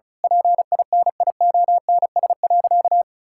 Morse code, heard as a keyed tone, about 36 wpm, 700 Hz. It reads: PINIONS.
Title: Pangram Challenge v2 - 35wpm